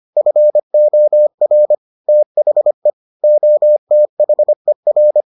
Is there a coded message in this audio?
Yes: Morse code from 0.2 to 5.2 s, heard as a keyed tone, about 25 wpm, 600 Hz. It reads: FOR THE OTHER